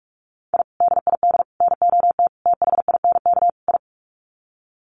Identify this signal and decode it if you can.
Morse code, heard as a keyed tone, about 44 wpm, 700 Hz. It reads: I DID NOT THINK I
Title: Sets of 5 Words Encore 45wpm